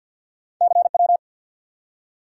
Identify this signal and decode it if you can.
Morse code, heard as a keyed tone, about 54 wpm, 700 Hz. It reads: CW
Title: Top 180 QSO Elements 50wpm